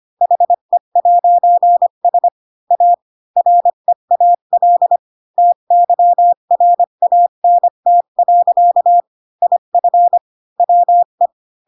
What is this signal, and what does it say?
Morse code, heard as a keyed tone, about 25 wpm, 700 Hz. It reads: HE'S A REAL TYRANT. IF WE
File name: Pinocchio / Pinocchio ch3 20 25 ik1hge